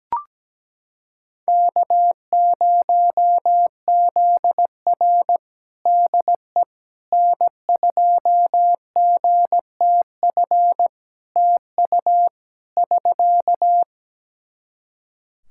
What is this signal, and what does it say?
Morse code, heard as a keyed tone, about 17 wpm, 700 Hz. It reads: K0ZR DE N2GTF TU <SK>
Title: Practice QSO - Exchange 4 - Ending - 17wpm